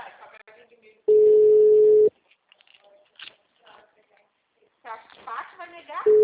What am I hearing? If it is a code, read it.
Ringback tone (Europe, 425 Hz), with speech.
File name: out- -2003- - - .90715